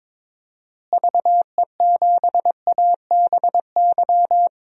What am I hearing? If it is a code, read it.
Morse code, heard as a keyed tone, about 22 wpm, 700 Hz. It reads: VE7ABY
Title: DXCC-100 Call Signs - 6 Characters - Repeated 2x - 22wpm